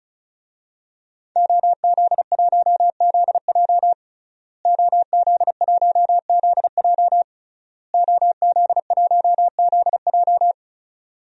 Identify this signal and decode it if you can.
Morse code, heard as a keyed tone, about 34 wpm, 700 Hz. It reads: OZ1ZJ OZ1ZJ OZ1ZJ
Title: DXCC-100 Call Signs - 5 Characters - Repeated 3x - 35wpm